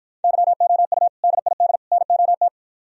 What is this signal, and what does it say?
Morse code, heard as a keyed tone, about 55 wpm, 700 Hz. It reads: YOU DID NOT